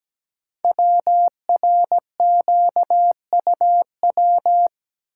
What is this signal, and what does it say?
Morse code, heard as a keyed tone, about 17 wpm, 700 Hz. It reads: WRQUW
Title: Code Groups - 5 Letters - 17wpm